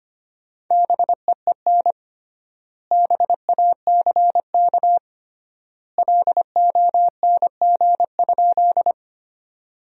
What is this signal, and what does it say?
Morse code, heard as a keyed tone, about 25 wpm, 700 Hz. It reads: BEEN BACK LONG?
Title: 3x Word-Spacing - Sentences from Top 200 Words - 25wpm